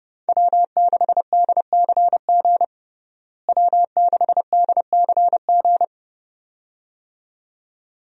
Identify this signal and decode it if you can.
Morse code, heard as a keyed tone, about 30 wpm, 700 Hz. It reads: W6DCG W6DCG
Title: USA Call Signs - 5 Characters - Repeated 2x - 30wpm